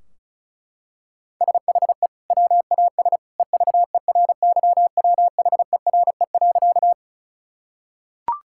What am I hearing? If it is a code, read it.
Morse code, heard as a keyed tone, about 34 wpm, 700 Hz. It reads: SHE WAS EVERYWHERE.
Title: Sentences from Top 1600 Words 35wpm